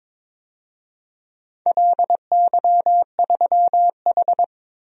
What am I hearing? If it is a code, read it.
Morse code, heard as a keyed tone, about 22 wpm, 700 Hz. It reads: LY3H
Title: DXCC-100 Call Signs - 4 Characters - 22wpm